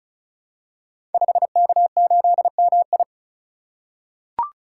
Morse code, heard as a keyed tone, about 36 wpm, 700 Hz. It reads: 5K8MI
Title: Code Groups - 5 Letter:Numbers - 35wpm